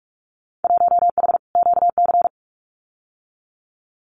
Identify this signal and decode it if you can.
Morse code, heard as a keyed tone, about 46 wpm, 700 Hz. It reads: 1H QC
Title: Winter Field Day Exchanges 45wpm